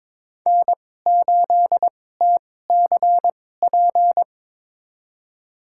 Morse code, heard as a keyed tone, about 23 wpm, spaced as for 16 wpm, 700 Hz. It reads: N8TCP
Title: USA Call Signs - 5 Characters - 2x Character Spacing - 22wpm